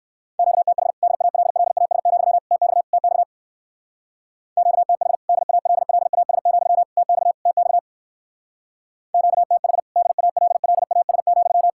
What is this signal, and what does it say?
Morse code, heard as a keyed tone, about 57 wpm, 700 Hz. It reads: QTH DALLAS, TX TX QTH DALLAS, TX TX QTH DALLAS,